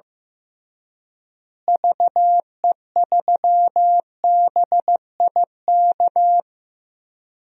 Morse code, heard as a keyed tone, about 15 wpm, 700 Hz. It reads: VE3BIK
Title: DXCC-100 Call Signs - 6 Characters - Repeated 3x - 15wpm